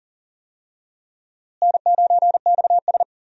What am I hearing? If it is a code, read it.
Morse code, heard as a keyed tone, about 40 wpm, 700 Hz. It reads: N9XS
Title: USA Call Signs - 4 Characters - Repeated 3x - 40wpm